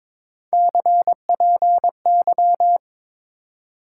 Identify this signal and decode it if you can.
Morse code, heard as a keyed tone, about 22 wpm, 700 Hz. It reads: CPY
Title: Top 90 QSO Elements 22wpm